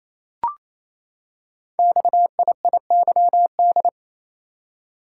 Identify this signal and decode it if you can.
Morse code, heard as a keyed tone, about 27 wpm, 700 Hz. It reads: XIIYD
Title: Code Groups - 5 Letter:Numbers - 28wpm